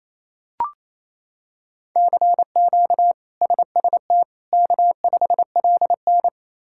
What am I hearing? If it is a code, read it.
Morse code, heard as a keyed tone, about 28 wpm, 700 Hz. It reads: CQ SST K5LN
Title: K1USN SST QSO Practice - 28wpm